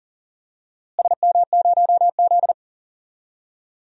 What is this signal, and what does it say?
Morse code, heard as a keyed tone, about 40 wpm, 700 Hz. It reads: SM0Z
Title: DXCC-100 Call Signs - 4 Characters - Repeated 3x - 40wpm